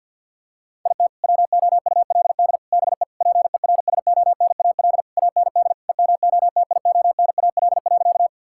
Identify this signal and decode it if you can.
Morse code, heard as a keyed tone, about 47 wpm, 690 Hz. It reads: IT WOULD BE PERSONAL AND EMOTIONAL.
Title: Sentences from Top 1000 Words - Review Part 5 - 50wpm